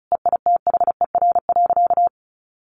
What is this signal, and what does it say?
Morse code, heard as a keyed tone, about 34 wpm, 700 Hz. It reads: EITHER.
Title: Sentences from Top 600 Words 35wpm